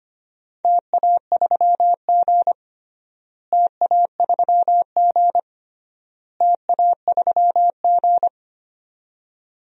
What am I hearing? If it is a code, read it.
Morse code, heard as a keyed tone, about 25 wpm, 700 Hz. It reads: TA3G TA3G TA3G
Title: DXCC-100 Call Signs - 4 Characters - Repeated 3x - 25wpm